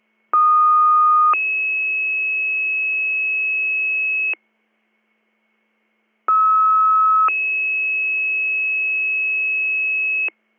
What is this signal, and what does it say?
Two-tone page, heard as a one-second tone, then a three-second tone, about 1230 then 2600 Hz, about 1290 then 2600 Hz. It is heard from 0.3 to 10.3 s.